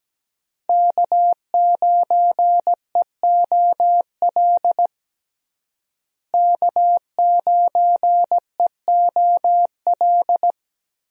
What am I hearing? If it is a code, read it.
Morse code, heard as a keyed tone, about 17 wpm, 700 Hz. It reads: K9EOL K9EOL
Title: USA Call Signs - 5 Characters - Repeated 2x - 17wpm